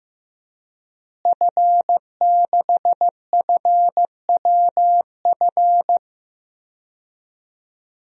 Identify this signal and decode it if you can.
Morse code, heard as a keyed tone, about 15 wpm, 700 Hz. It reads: F6FWF
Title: DXCC-100 Call Signs - 5 Characters - Repeated 2x - 15wpm